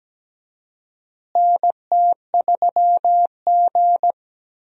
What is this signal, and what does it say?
Morse code, heard as a keyed tone, about 17 wpm, 700 Hz. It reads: NT3G
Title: USA Call Signs - 4 Characters - Repeated 2x - 17wpm